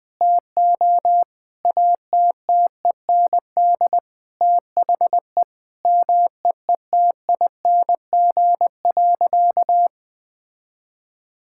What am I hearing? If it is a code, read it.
Morse code, heard as a keyed tone, about 20 wpm, 700 Hz. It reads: TO ATTEND THE MEETING.
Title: Sentences with 6-Word Phrases - 20wpm